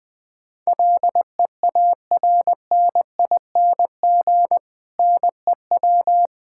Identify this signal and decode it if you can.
Morse code, heard as a keyed tone, about 20 wpm, 700 Hz. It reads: LEARNING NEW
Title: Difficult Sentences with Hints - 20wpm